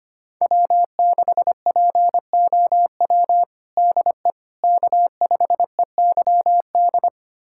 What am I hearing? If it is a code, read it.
Morse code, heard as a keyed tone, about 25 wpm, 700 Hz. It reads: W6POW DE K5EYD
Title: USA Callsign DE Callsign 25wpm